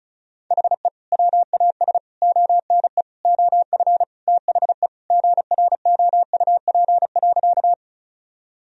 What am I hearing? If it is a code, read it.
Morse code, heard as a keyed tone, about 34 wpm, 700 Hz. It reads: HE WAS ONE OF THE GROUP.
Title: Sentences from Top 1000 Words - Review Part 1 - 35wpm